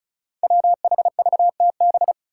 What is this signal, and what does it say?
Morse code, heard as a keyed tone, about 34 wpm, 700 Hz. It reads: WHVTB
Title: Five Challenging Letter-Number Combinations 35wpm